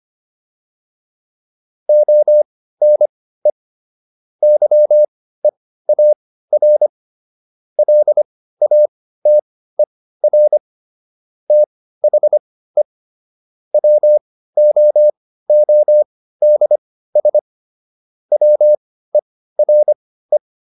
Morse code, heard as a keyed tone, about 25 wpm, spaced as for 15 wpm, 600 Hz. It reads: ONE YEAR LATER THE WOODS WERE